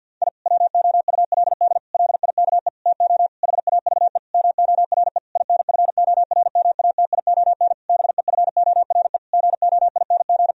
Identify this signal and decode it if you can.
Morse code, heard as a keyed tone, about 50 wpm, 700 Hz. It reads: I WOULD LIKE TO HAVE MORE INFORMATION BEFORE GOING
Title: Sentences from Top 1000 Words - Review Part 3 - 50wpm